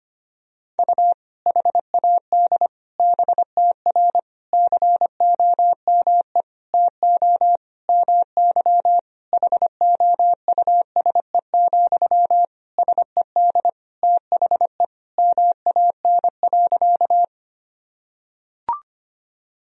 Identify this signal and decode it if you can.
Morse code, heard as a keyed tone, about 25 wpm, 700 Hz. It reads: U HAD BTR COME TO MY HOUSE, SED THE MAN.
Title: Sentences from Top 400 Words and Abbreviations 25wpm